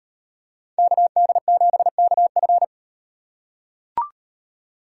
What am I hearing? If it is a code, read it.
Morse code, heard as a keyed tone, about 36 wpm, 700 Hz. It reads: KD7KF